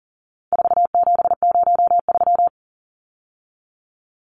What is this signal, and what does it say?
Morse code, heard as a keyed tone, about 39 wpm, 700 Hz. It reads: 4703
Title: Numbers - 4-Digit 40wpm